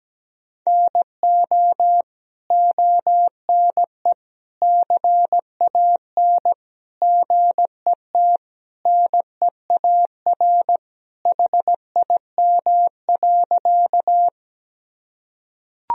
Morse code, heard as a keyed tone, about 17 wpm, 700 Hz. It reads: NO ONE CAN GET NEAR HIM.
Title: Sentences from Top 1000 Words - Review Part 1 - 17wpm